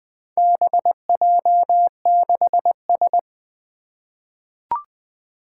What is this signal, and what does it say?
Morse code, heard as a keyed tone, about 20 wpm, 700 Hz. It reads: BJ6S
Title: Four Challenging Letter-Number Combinations 20wpm